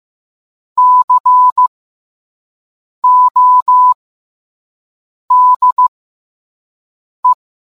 Morse code, heard as a keyed tone, about 15 wpm, spaced as for 5 wpm, 1000 Hz. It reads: CODE